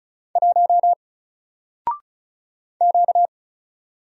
Morse code, heard as a keyed tone, about 36 wpm, 700 Hz. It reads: 1 Q
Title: Single Letter-Number Alt 35wpm